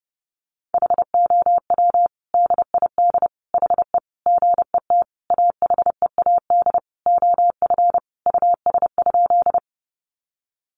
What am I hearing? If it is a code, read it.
Morse code, heard as a keyed tone, about 30 wpm, 700 Hz. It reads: HOW DID HE GET AHEAD OF US?
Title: Sentences from Top 1100 Words 30wpm